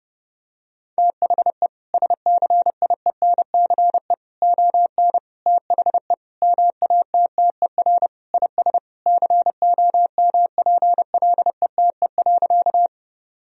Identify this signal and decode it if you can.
Morse code, heard as a keyed tone, about 30 wpm, 700 Hz. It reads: THE SCIENCE ON THE MATTER IS COMPLETE.
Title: Sentences from Top 1000 Words - Review Part 2 - 30wpm